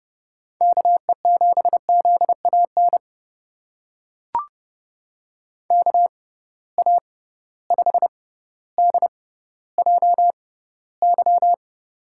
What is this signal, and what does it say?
Morse code, heard as a keyed tone, about 30 wpm, spaced as for 15 wpm, 700 Hz. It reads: KE7ZAN KA5DJY
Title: USA Call Signs - 6x Character Spacing - 30wpm